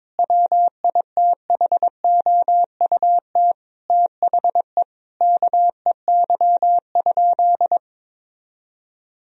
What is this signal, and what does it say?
Morse code, heard as a keyed tone, about 22 wpm, 700 Hz. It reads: WITHOUT THE KEY?
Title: Sentences from Top 1000 Words - Review Part 4 - 22wpm